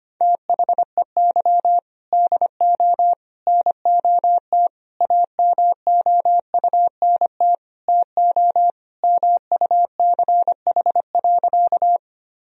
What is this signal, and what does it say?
Morse code, heard as a keyed tone, about 25 wpm, 700 Hz. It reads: THEY DO NOT AMOUNT TO MUCH.
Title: Sentences from Top 1000 Words - Review Part 4 - 25wpm